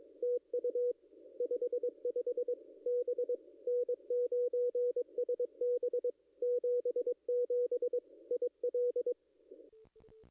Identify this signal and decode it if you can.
Morse code, heard as a keyed tone, about 22 wpm, 490 Hz. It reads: TU 55 B N9SB 77 IL
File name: contests / ss-cw-k5tr-op-n5rz -0100